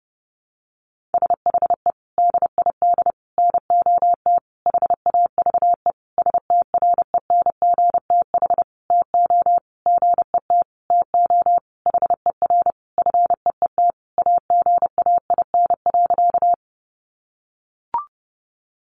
Morse code, heard as a keyed tone, about 30 wpm, 700 Hz. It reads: SHE DID NOT HAVE STRENGTH TO GET TO HER FEET AGAIN.
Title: Sentences from Top 1300 Words 30wpm